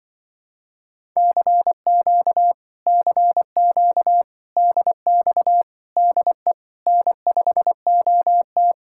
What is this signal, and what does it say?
Morse code, heard as a keyed tone, about 23 wpm, 700 Hz. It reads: CQ CQ DX DE N5OT
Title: Speed-Racing Common QSO Phrases - 20wpm